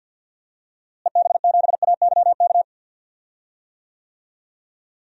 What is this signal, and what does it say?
Morse code, heard as a keyed tone, about 52 wpm, 700 Hz. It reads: EB7AYK
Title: DXCC-100 Call Signs - 6 Characters - Repeated 3x - 50wpm